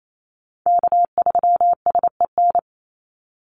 Morse code, heard as a keyed tone, about 28 wpm, 700 Hz. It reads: K3SEN
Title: Call Signs Repeated 2x - 28wpm